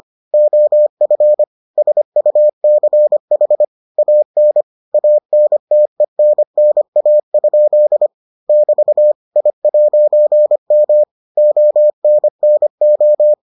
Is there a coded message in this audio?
Yes: Morse code from 0.3 to 13.3 s, heard as a keyed tone, about 25 wpm, 600 Hz. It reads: OF SUCH AN ANTENNA? <BT> I'M ONNO